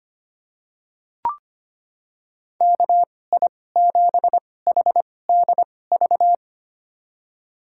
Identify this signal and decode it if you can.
Morse code, heard as a keyed tone, about 25 wpm, spaced as for 18 wpm, 700 Hz. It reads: KI7HDV
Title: USA Call Signs - 6 Characters - 2x Character Spacing - 25wpm